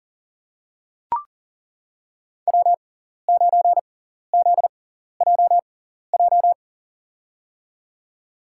Morse code, heard as a keyed tone, about 40 wpm, spaced as for 14 wpm, 700 Hz. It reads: W9ZJJ
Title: USA Call Signs - 5 Characters - 6x Character Spacing - 40wpm